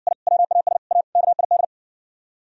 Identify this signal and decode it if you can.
Morse code, heard as a keyed tone, about 59 wpm, 700 Hz. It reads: I WAS A KID